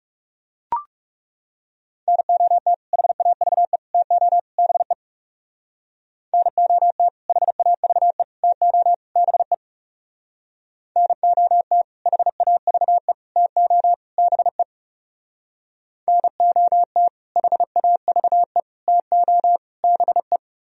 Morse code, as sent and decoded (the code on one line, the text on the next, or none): -. --- - / .... .- ...- . / - --- / -... / -. --- - / .... .- ...- . / - --- / -... / -. --- - / .... .- ...- . / - --- / -... / -. --- - / .... .- ...- . / - --- / -... .
NOT HAVE TO BE NOT HAVE TO BE NOT HAVE TO BE NOT HAVE TO BE